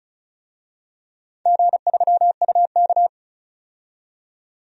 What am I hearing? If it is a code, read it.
Morse code, heard as a keyed tone, about 36 wpm, 690 Hz. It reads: G3UK
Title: DXCC-100 Call Signs - 4 Characters - 35wpm